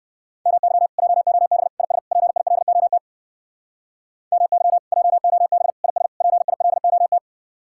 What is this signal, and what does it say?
Morse code, heard as a keyed tone, about 56 wpm, 700 Hz. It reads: MY JOB IS PILOT MY JOB IS PILOT
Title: Speed-Racing Common QSO Phrases Encore - 45wpm